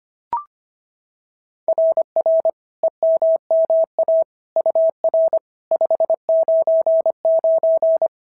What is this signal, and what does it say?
Morse code, heard as a keyed tone, about 22 wpm, 650 Hz. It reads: RR EMMA UR 599
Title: SOTA QSO Practice 25wpm